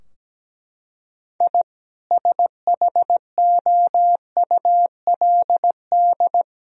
Morse code, heard as a keyed tone, about 17 wpm, 700 Hz. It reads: I SHOULD
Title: Sentences from Top 1000 Words - Review Part 3 - 17wpm